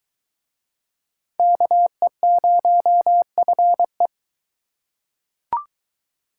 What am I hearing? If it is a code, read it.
Morse code, heard as a keyed tone, about 22 wpm, 700 Hz. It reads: KE0FE